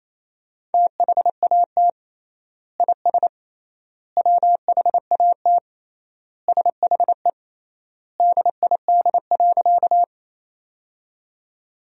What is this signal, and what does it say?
Morse code, heard as a keyed tone, about 28 wpm, 700 Hz. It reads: THAT IS WHAT SHE DID.
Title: 3x Word-Spacing - Sentences from Top 100 Words - 28wpm